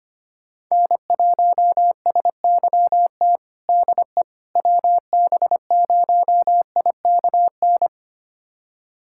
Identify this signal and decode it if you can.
Morse code, heard as a keyed tone, about 25 wpm, 700 Hz. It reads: N1SYT DE WB0IKN
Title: USA Callsign DE Callsign 25wpm